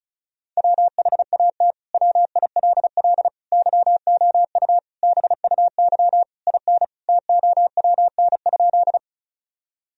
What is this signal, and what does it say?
Morse code, heard as a keyed tone, about 36 wpm, 700 Hz. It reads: WHAT WILL YOU BUY IN TOWN?
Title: Sentences from Top 1000 Words - Review Part 4 - 35wpm